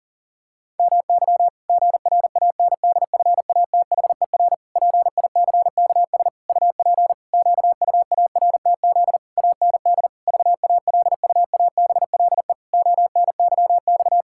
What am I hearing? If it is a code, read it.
Morse code, heard as a keyed tone, about 40 wpm, 700 Hz. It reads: MY GRANDFATHER PICKS UP QUARTZ AND VALUABLE ONYX
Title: Pangram Challenge v2 - 40wpm